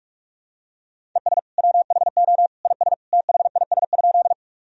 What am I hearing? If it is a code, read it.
Morse code, heard as a keyed tone, about 46 wpm, 700 Hz. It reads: ES WHO IS THIS?
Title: Sentences from Top 100 Words and Abbreviations 45wpm